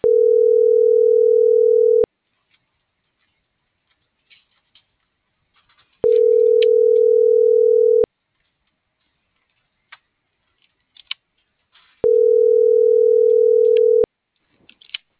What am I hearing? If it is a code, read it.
Ringback tone (North America), 440 + 480 Hz.